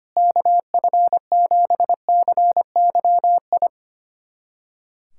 Morse code, heard as a keyed tone, about 25 wpm, 700 Hz. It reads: KF7CYI
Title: USA Callsign DE Callsign - 6 Characters - 25wpm